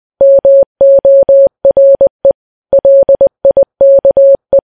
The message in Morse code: MORE LIKE